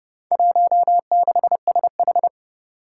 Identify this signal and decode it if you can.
Morse code, heard as a keyed tone, about 30 wpm, 700 Hz. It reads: 16SH
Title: Four Challenging Letter-Number Combinations 30wpm